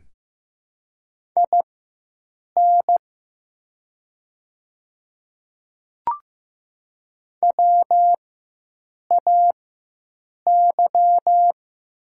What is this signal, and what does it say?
Morse code, heard as a keyed tone, about 15 wpm, spaced as for 7 wpm, 700 Hz. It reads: IN WAY
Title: 4x Character Spacing - Top 100 Words - 15wpm